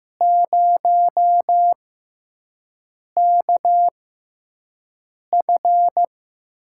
Morse code, heard as a keyed tone, about 16 wpm, spaced as for 5 wpm, 700 Hz. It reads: 0KF
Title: USA Call Signs - 6 Characters - 6x Character Spacing - 15wpm